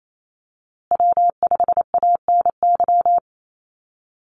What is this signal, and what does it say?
Morse code, heard as a keyed tone, about 28 wpm, 700 Hz. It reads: W5ANY
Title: USA Call Signs Repeated 3x 28wpm